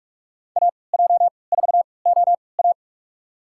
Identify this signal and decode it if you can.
Morse code, heard as a keyed tone, about 45 wpm, spaced as for 25 wpm, 700 Hz. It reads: AJ4OA
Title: USA Call Signs - 5 Characters - 3x Character Spacing - 45wpm